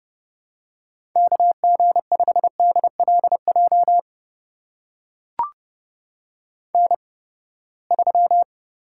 Morse code, heard as a keyed tone, about 30 wpm, spaced as for 21 wpm, 700 Hz. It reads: KG5DLJ N3
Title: USA Call Signs - 8x Character Spacing - 30wpm